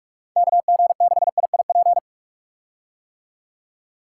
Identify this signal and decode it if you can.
Morse code, heard as a keyed tone, about 47 wpm, 700 Hz. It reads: KG6IIP
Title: K1USN SST QSO Practice - 45wpm